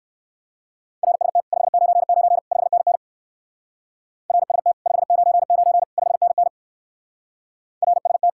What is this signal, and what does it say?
Morse code, heard as a keyed tone, about 57 wpm, 700 Hz. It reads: RST 599 5NN RST 599 5NN RST